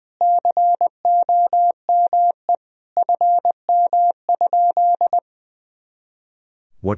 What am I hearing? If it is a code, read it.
Morse code, heard as a keyed tone, about 20 wpm, 700 Hz. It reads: COME FM?
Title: Sentences from Top 300 Words and Abbreviations 20wpm